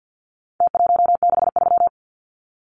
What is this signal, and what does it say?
Morse code, heard as a keyed tone, about 48 wpm, 700 Hz. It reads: T163
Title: Numbers - Contest Style - 4-Digit 50wpm